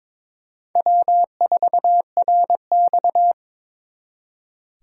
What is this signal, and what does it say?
Morse code, heard as a keyed tone, about 25 wpm, 700 Hz. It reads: W4RX
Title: Speed Racing - USA 1x2 Call Signs 22wpm